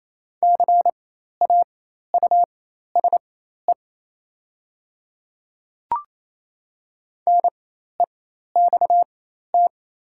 Morse code, heard as a keyed tone, about 28 wpm, spaced as for 13 wpm, 690 Hz. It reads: CAUSE NEXT
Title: 4x Character Spacing - Top 300 Words - 28wpm